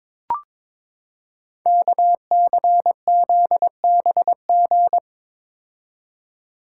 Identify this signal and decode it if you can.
Morse code, heard as a keyed tone, about 22 wpm, 700 Hz. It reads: KCZBG
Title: Code Groups - 5 Letters - 22wpm